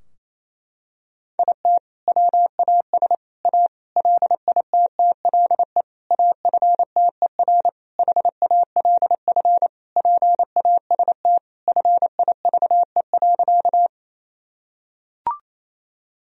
Morse code, heard as a keyed tone, about 28 wpm, 700 Hz. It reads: IT WAS A LITTLE AFTER HALF PAST FIVE.
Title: Sentences from Top 1000 Words - Review Part 3 - 28wpm